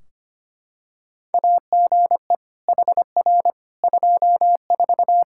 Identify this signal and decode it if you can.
Morse code, heard as a keyed tone, about 25 wpm, 700 Hz. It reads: AGE HR 24
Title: K1USN SST QSO Practice 25wpm